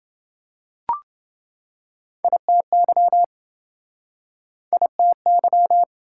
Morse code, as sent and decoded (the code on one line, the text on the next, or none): .. - -.-- / .. - -.--
ITY ITY